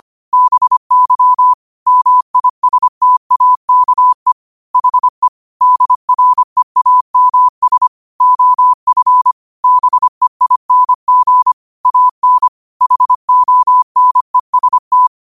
Morse code, heard as a keyed tone, about 25 wpm, 1000 Hz. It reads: BY MISTAKE HE DREAMS OF BEING AN HONEST
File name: qotdfiles / 25wpm